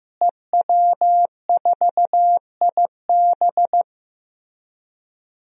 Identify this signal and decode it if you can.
Morse code, heard as a keyed tone, about 16 wpm, 700 Hz. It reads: EW4IB
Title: DXCC-100 Call Signs - 5 Characters - Repeated 3x - 15wpm